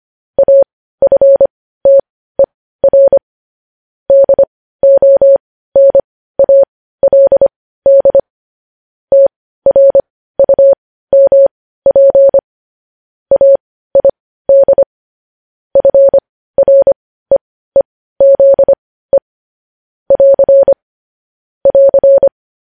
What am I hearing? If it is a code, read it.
Morse code, heard as a keyed tone, about 25 wpm, spaced as for 15 wpm, 550 Hz. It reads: AFTER DONALD TRUMP AID FREEZE <AR> <AR>
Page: news-headlines-in-morse-code-at-15-wpm